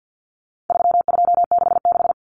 Morse code, heard as a keyed tone, about 50 wpm, 700 Hz. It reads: 3266
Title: Numbers - 4-Digit 50wpm